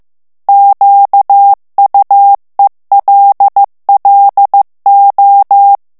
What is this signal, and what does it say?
Morse code, heard as a keyed tone, about 15 wpm, 800 Hz. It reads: QUELLO
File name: morse mp3 / ITALIEN texte clair 2 vitesse 14 mots minute